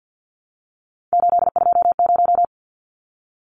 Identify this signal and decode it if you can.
Morse code, heard as a keyed tone, about 50 wpm, 700 Hz. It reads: 820
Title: Numbers - 3-Digit 50wpm